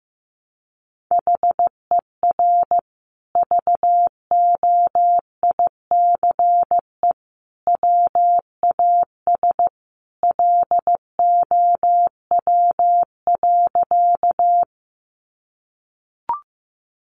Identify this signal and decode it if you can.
Morse code, heard as a keyed tone, about 15 wpm, 700 Hz. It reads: HER VOICE WAS LOW.
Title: Sentences from Top 500 Words 15wpm